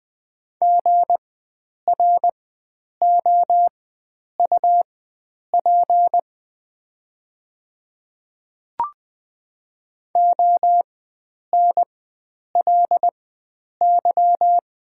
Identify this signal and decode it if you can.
Morse code, heard as a keyed tone, about 20 wpm, spaced as for 9 wpm, 700 Hz. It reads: GROUP ONLY
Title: 4x Character Spacing - Top 500 Words - 20wpm